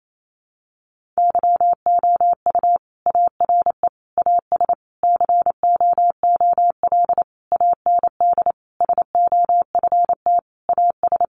Morse code, heard as a keyed tone, about 28 wpm, 700 Hz. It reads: YOU ARE AS COOL AND SOFT AS
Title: Sentences from Top 1500 Words 28wpm